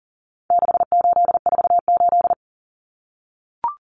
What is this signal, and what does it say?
Morse code, heard as a keyed tone, about 40 wpm, 700 Hz. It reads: 6848
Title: Numbers - 4-Digit 40wpm